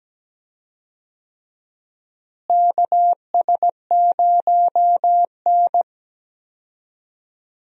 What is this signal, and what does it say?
Morse code, heard as a keyed tone, about 17 wpm, 700 Hz. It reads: KS0N
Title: USA Call Signs - 4 Characters - Repeated 2x - 17wpm